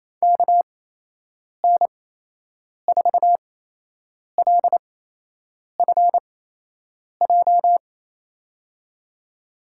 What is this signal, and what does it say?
Morse code, heard as a keyed tone, about 28 wpm, spaced as for 8 wpm, 700 Hz. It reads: KN4LFJ
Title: USA Call Signs - 6 Characters - 8x Character Spacing - 28wpm